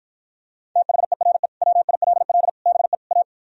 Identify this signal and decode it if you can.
Morse code, heard as a keyed tone, about 52 wpm, 700 Hz. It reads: THERE WILL BE A